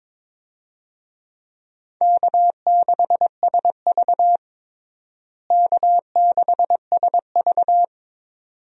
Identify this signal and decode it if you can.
Morse code, heard as a keyed tone, about 22 wpm, 700 Hz. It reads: K6SV K6SV
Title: USA Call Signs - 4 Characters - Repeated 2x - 22wpm